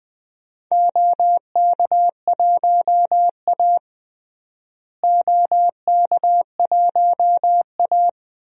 Morse code, heard as a keyed tone, about 20 wpm, 700 Hz. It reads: OK1A OK1A
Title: DXCC-100 Call Signs - 4 Characters - Repeated 2x - 20wpm